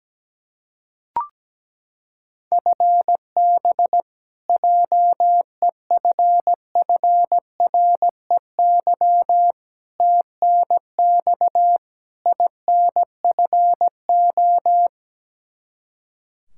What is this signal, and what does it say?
Morse code, heard as a keyed tone, about 17 wpm, 700 Hz. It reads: FB JEFFREY TNX INFO
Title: K1USN SST QSO Practice 17wpm